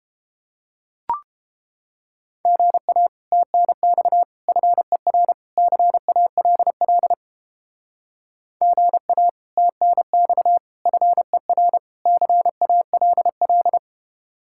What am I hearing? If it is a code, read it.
Morse code, heard as a keyed tone, about 31 wpm, 700 Hz. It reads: GA TNX FER CALL GA TNX FER CALL